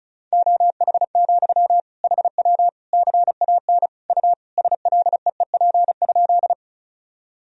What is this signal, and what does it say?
Morse code, heard as a keyed tone, about 36 wpm, 700 Hz. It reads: OH, HW CAN U SLEEP?
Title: Sentences from Top 500 Words and Abbreviations 35wpm